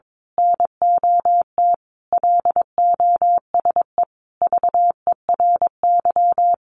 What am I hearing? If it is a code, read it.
Morse code, heard as a keyed tone, about 22 wpm, 700 Hz. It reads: NOT LOSE VERY
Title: Sentences from Top 1100 Words 22wpm